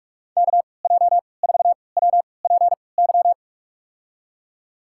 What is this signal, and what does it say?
Morse code, heard as a keyed tone, about 47 wpm, spaced as for 26 wpm, 700 Hz. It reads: KJ4WPY
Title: USA Call Signs - 6 Characters - 3x Character Spacing - 45wpm